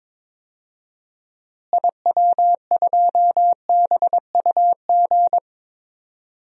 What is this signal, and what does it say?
Morse code, heard as a keyed tone, about 22 wpm, 700 Hz. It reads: IW2BUG
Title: DXCC-100 Call Signs - 6 Characters - Repeated 2x - 22wpm